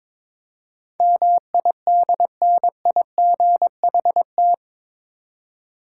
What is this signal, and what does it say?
Morse code, heard as a keyed tone, about 22 wpm, 700 Hz. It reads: MIDNIGHT
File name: practice / 22wpm / Eight Letter Words - 22wpm